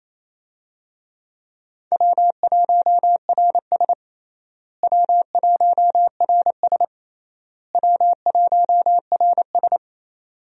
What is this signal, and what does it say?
Morse code, heard as a keyed tone, about 28 wpm, 700 Hz. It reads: W1RS W1RS W1RS
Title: USA Call Signs - 4 Characters - Repeated 3x - 28wpm